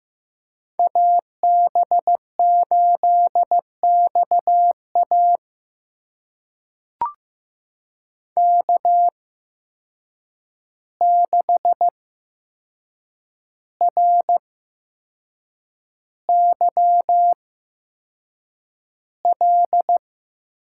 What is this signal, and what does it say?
Morse code, heard as a keyed tone, about 15 wpm, spaced as for 6 wpm, 700 Hz. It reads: AB8XA K6RYL
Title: USA Call Signs - 5 Characters - 8x Character Spacing - 15wpm